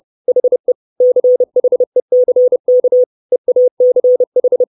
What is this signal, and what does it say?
Morse code, heard as a keyed tone, about 30 wpm, 500 Hz. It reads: HE CHECK EACH
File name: 632 Words 30WPM